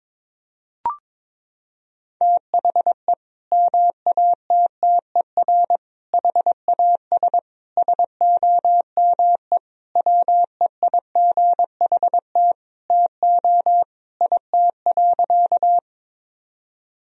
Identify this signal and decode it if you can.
Morse code, heard as a keyed tone, about 22 wpm, 700 Hz. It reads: THE MATTER HAS SOME WEIGHT TO IT.
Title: Sentences from Top 1000 Words - Review Part 3 - 22wpm